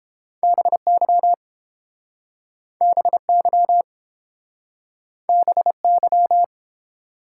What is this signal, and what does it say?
Morse code, heard as a keyed tone, about 29 wpm, 700 Hz. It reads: BY BY BY